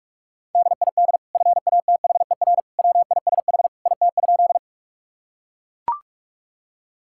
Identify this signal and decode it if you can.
Morse code, heard as a keyed tone, about 41 wpm, 700 Hz. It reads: DID FATHER WISH IT?
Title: Sentences from Top 1600 Words 45wpm